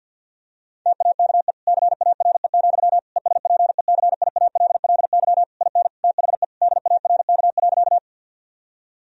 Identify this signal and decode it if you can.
Morse code, heard as a keyed tone, about 50 wpm, 700 Hz. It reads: TAKE CARE, ESPECIALLY IN THE DARK.
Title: Sentences from Top 1000 Words - Review Part 4 - 50wpm